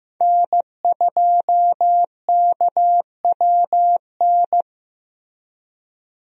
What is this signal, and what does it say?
Morse code, heard as a keyed tone, about 16 wpm, 700 Hz. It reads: N2KWN
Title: USA Call Signs - 5 Characters - Repeated 3x - 15wpm